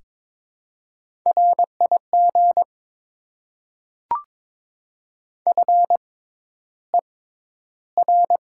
Morse code, heard as a keyed tone, about 22 wpm, spaced as for 11 wpm, 700 Hz. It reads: RIG FER
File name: Top 20 QSO Elements - 6x Character Spacing - 22wpm